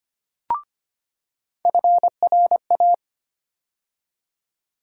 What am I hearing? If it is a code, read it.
Morse code, heard as a keyed tone, about 25 wpm, 700 Hz. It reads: FRA
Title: USA Call Sign Postfixes - 3 Letters 25wpm